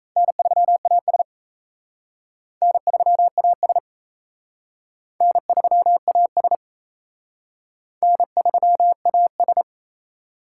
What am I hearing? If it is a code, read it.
Morse code, heard as a keyed tone, about 35 wpm, 700 Hz. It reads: N3AS N3AS N3AS N3AS